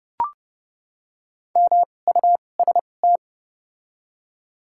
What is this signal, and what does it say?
Morse code, heard as a keyed tone, about 29 wpm, spaced as for 21 wpm, 700 Hz. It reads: MUST
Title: 2x Character Spacing - Top 300 Words - 30wpm